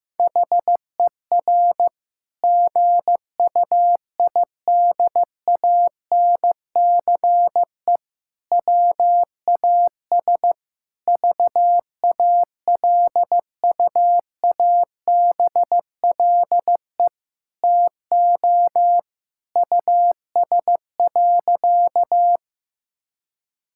Morse code, heard as a keyed tone, about 16 wpm, 700 Hz. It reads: HER GUIDANCE WAS VALUABLE TO US.
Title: Sentences from Top 1600 Words 15wpm